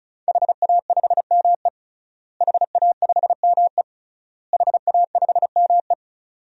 Morse code, heard as a keyed tone, about 36 wpm, 700 Hz. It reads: HA5ME HA5ME HA5ME
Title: DXCC-100 Call Signs - 5 Characters - Repeated 3x - 35wpm